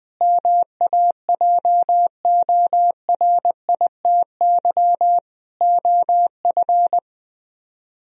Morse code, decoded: MAJORITY OF

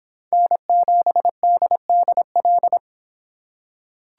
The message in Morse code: N7DDL

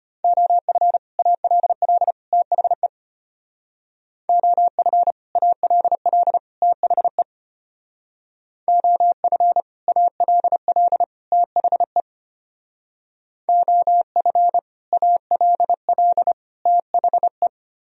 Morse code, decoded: OF ALL THE OF ALL THE OF ALL THE OF ALL THE

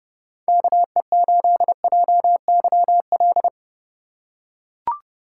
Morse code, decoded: KE8JYL

